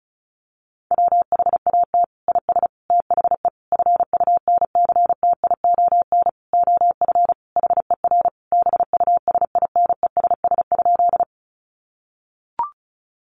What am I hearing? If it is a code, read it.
Morse code, heard as a keyed tone, about 34 wpm, 700 Hz. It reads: WHAT IS THE FUNCTION OF HER BUSINESS?
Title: Sentences from Top 1100 Words 35wpm